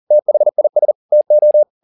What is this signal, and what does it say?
Morse code, heard as a keyed tone, about 40 wpm, 600 Hz. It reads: THIS TO